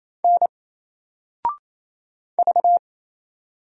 Morse code, heard as a keyed tone, about 27 wpm, 700 Hz. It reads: N V
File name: Single-Letter-Number-28wpm